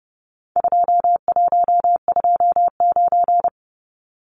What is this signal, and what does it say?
Morse code, heard as a keyed tone, about 30 wpm, 700 Hz. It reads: 2129
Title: Numbers - 4-Digit 30wpm